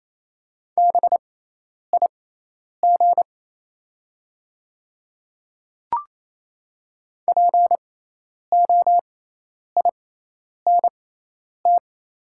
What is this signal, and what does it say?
Morse code, heard as a keyed tone, about 28 wpm, spaced as for 10 wpm, 700 Hz. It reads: BIG POINT
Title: 6x Character Spacing - Top 300 Words - 28wpm